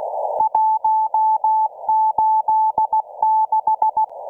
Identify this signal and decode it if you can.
Morse code, heard as a keyed tone, about 16 wpm, 870 Hz. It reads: 186